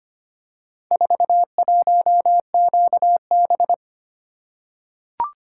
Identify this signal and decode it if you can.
Morse code, heard as a keyed tone, about 25 wpm, 700 Hz. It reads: 41QB
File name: practice / 25wpm / Four Letter-Number Combinations - 25wpm